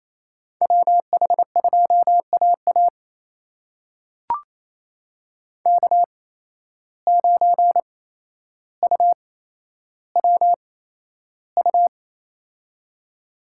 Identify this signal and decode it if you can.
Morse code, heard as a keyed tone, about 28 wpm, spaced as for 12 wpm, 700 Hz. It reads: WH2AA K9UWU
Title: USA Call Signs - 5 Characters - 8x Character Spacing - 28wpm